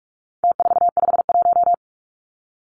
Morse code, heard as a keyed tone, about 44 wpm, 700 Hz. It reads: T451
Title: Numbers - Contest Style - 4-Digit 45wpm